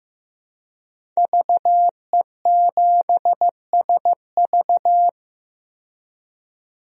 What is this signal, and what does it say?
Morse code, heard as a keyed tone, about 15 wpm, 700 Hz. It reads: VE7SV
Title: DXCC-100 Call Signs - 5 Characters - 15wpm